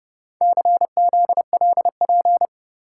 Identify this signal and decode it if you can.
Morse code, heard as a keyed tone, about 29 wpm, 700 Hz. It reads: CZLP